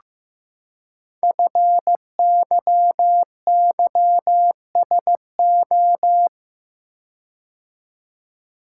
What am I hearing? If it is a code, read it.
Morse code, heard as a keyed tone, about 15 wpm, 700 Hz. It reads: FYYSO
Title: Code Groups - 5 Letters - 15wpm